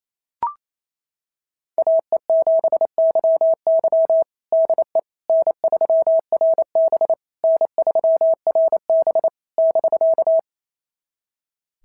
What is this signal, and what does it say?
Morse code, heard as a keyed tone, about 28 wpm, 690 Hz. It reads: AE7YY DE N3RB N3RB <BK>
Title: SOTA QSO Practice 28wpm